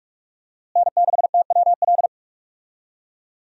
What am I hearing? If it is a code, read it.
Morse code, heard as a keyed tone, about 46 wpm, 700 Hz. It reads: N6TWL
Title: USA Call Signs - 5 Characters - Repeated 3x - 45wpm